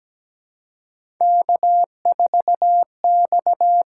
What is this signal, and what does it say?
Morse code, heard as a keyed tone, about 17 wpm, 700 Hz. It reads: K4X
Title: USA Call Signs - 6x Character Spacing - 17wpm